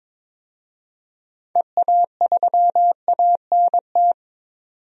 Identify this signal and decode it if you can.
Morse code, heard as a keyed tone, about 22 wpm, 700 Hz. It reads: EA3ANT
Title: DXCC-100 Call Signs - 6 Characters - 22wpm